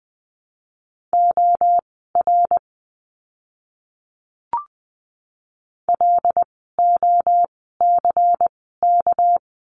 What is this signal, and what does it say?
Morse code, heard as a keyed tone, about 20 wpm, spaced as for 14 wpm, 700 Hz. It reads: OR LOCK
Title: 199 Lesson 32f - Letter k - Words 20wpm